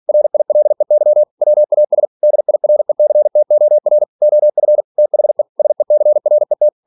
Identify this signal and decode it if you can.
Morse code, heard as a keyed tone, about 46 wpm, 600 Hz. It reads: RILEY WAS DIRECTOR OF THE SECRET
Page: Never 47wpm